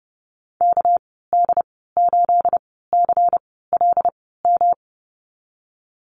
Morse code, heard as a keyed tone, about 31 wpm, spaced as for 17 wpm, 700 Hz. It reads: KD8CLM
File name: Lesson 37g - Letter x - Call Signs 30wpm